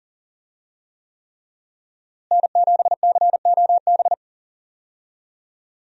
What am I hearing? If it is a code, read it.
Morse code, heard as a keyed tone, about 38 wpm, 700 Hz. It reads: N7COB
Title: USA Call Signs - 5 Characters - Repeated 3x - 40wpm